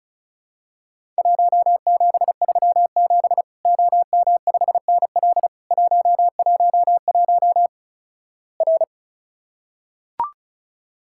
Morse code, heard as a keyed tone, about 36 wpm, 700 Hz. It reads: 1737 OM5NL 111 R
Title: Worked All Europe DX Contest - QTC Exchange - 35wpm